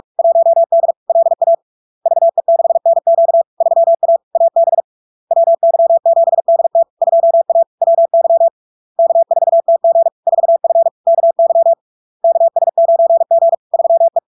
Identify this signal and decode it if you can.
Morse code, heard as a keyed tone, about 46 wpm, 670 Hz. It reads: 1D LA VE6NQ 3A AB WY7DT 2A WY K4TG 4F KY KS9G 3E
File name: LARK-FD-2022-Log-in-CW-45wpm-670-Hz